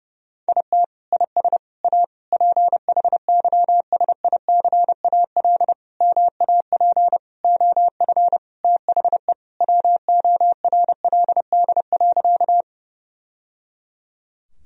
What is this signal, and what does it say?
Morse code, heard as a keyed tone, about 30 wpm, 700 Hz. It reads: IT IS A PHYSICAL MAP OF THE WORLD.
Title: Sentences from Top 1000 Words - Review Part 4 - 30wpm